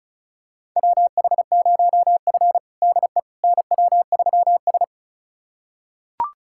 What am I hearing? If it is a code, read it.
Morse code, heard as a keyed tone, about 36 wpm, 700 Hz. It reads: WH0F DE NW3S
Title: USA Callsign DE Callsign - 4 Characters - 35wpm